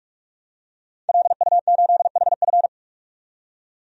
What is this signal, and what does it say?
Morse code, heard as a keyed tone, about 43 wpm, 700 Hz. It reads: LU8HF
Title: DXCC-100 Call Signs - 5 Characters - 45wpm